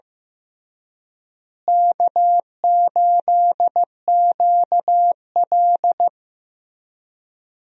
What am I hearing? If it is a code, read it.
Morse code, heard as a keyed tone, about 15 wpm, 700 Hz. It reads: K8QL
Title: USA Call Signs - 4 Characters - Repeated 3x - 15wpm